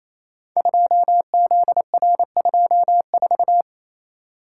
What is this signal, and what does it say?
Morse code, heard as a keyed tone, about 28 wpm, 700 Hz. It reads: 2ZR24
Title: Code Groups - 5 Letter:Numbers - 28wpm